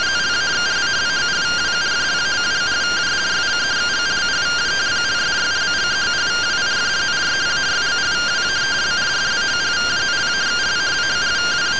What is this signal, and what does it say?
Two-tone teleprinter signal, about 170 Hz shift.